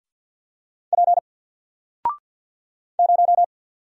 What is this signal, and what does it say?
Morse code, heard as a keyed tone, about 48 wpm, 700 Hz. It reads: P 0
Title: Single Character (Letter:Number:Punctuation:Prosign) 50wpm